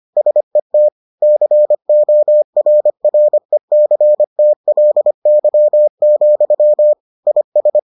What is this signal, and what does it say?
Morse code, heard as a keyed tone, about 25 wpm, 600 Hz. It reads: SET CORRECTLY, IS